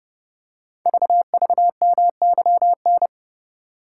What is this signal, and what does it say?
Morse code, heard as a keyed tone, about 28 wpm, 700 Hz. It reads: VVMYN
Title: Code Groups - 5 Letters - 30wpm